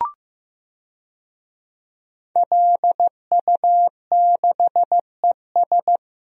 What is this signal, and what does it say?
Morse code, heard as a keyed tone, about 15 wpm, 700 Hz. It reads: LU6ES
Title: DXCC-100 Call Signs - 5 Characters - Repeated 2x - 15wpm